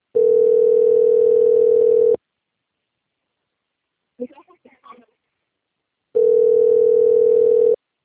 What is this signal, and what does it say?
Ringback tone (North America), 440 + 480 Hz, with speech.
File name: out- - - - - .28573